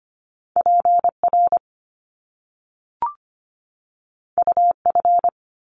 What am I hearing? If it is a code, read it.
Morse code, heard as a keyed tone, about 25 wpm, 700 Hz. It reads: PR UF
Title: Top Two Letter Combinations in Words 25wpm